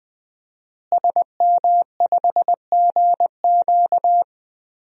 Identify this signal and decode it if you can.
Morse code, heard as a keyed tone, about 20 wpm, 700 Hz. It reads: SM5GQ
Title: DXCC-100 Call Signs - 5 Characters - Repeated 3x - 20wpm